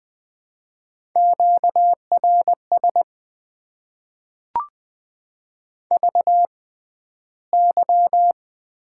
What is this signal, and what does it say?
Morse code, heard as a keyed tone, about 20 wpm, spaced as for 12 wpm, 700 Hz. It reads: QRS VY
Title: Top 90 QSO Elements - 6x Character Spacing - 20wpm